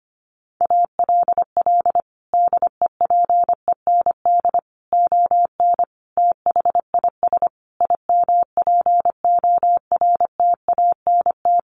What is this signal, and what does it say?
Morse code, heard as a keyed tone, about 25 wpm, 700 Hz. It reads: ALL DEPEND ON THIS IMPORTANT
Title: Sentences from Top 1300 Words 25wpm